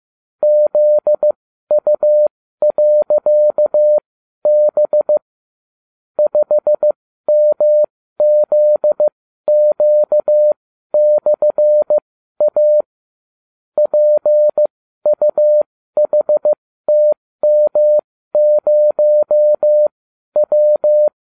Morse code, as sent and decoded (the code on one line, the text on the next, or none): --.. ..- .-.-.- -... / ..... -- --.. --.- -..-. .- / .--. ..- .... - -- ----- .--
ZU.B 5MZQ/A PUHTM0W